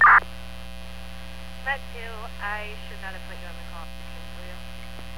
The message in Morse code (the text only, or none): none